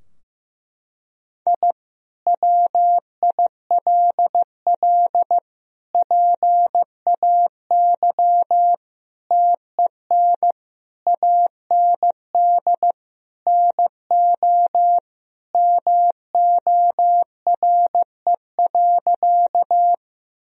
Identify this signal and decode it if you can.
Morse code, heard as a keyed tone, about 15 wpm, 700 Hz. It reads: I WILL PAY TEN AND NO MORE.
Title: Sentences from Top 1000 Words - Review Part 2 - 15wpm